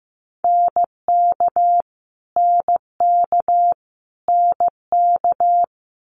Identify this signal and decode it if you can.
Morse code, heard as a keyed tone, about 15 wpm, 700 Hz. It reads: NK NK NK
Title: Multi-letter Phonemes Repeated 3x 15wpm